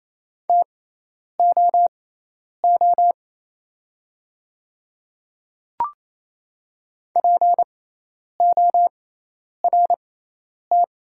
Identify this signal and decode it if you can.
Morse code, heard as a keyed tone, about 28 wpm, spaced as for 10 wpm, 690 Hz. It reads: TOO PORT
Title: 6x Character Spacing - Top 200 Words - 28wpm